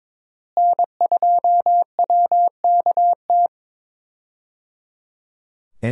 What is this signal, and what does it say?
Morse code, heard as a keyed tone, about 22 wpm, 700 Hz. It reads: N2WKT
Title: USA Call Signs - 5 Characters - Repeated 3x - 22wpm